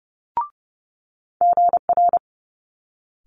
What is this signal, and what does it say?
Morse code, heard as a keyed tone, about 30 wpm, 700 Hz. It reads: GR
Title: Multi-letter Phonemes Quick Fire 30wpm